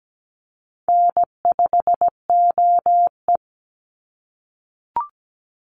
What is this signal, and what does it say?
Morse code, heard as a keyed tone, about 17 wpm, 700 Hz. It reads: N5OE
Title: Easy Call Signs - 17wpm